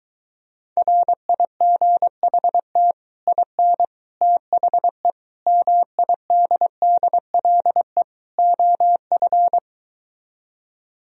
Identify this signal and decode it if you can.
Morse code, heard as a keyed tone, about 23 wpm, 700 Hz. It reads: RIGHT IN THE MIDDLE OF